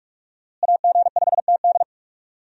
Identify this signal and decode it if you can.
Morse code, heard as a keyed tone, about 47 wpm, 700 Hz. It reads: AG5TD
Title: USA Call Signs - 5 Characters - Repeated 2x - 45wpm